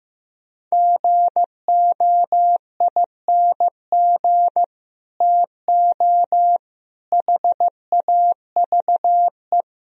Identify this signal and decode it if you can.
Morse code, heard as a keyed tone, about 15 wpm, 700 Hz. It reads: GOING TO HAVE
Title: Speed-Racing - Sets of 5 Words 15wpm